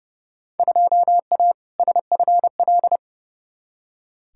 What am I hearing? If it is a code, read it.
Morse code, heard as a keyed tone, about 30 wpm, 700 Hz. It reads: 2A SFL
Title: Field Day Exchanges Encore - 30wpm